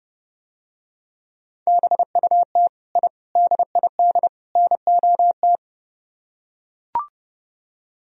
Morse code, heard as a keyed tone, about 32 wpm, 700 Hz. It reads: BUT I DID NOT